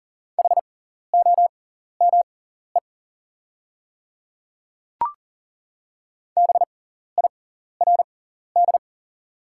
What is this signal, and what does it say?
Morse code, heard as a keyed tone, about 40 wpm, spaced as for 14 wpm, 700 Hz. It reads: HOME BIRD
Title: 6x Character Spacing - Top 500 Words - 40wpm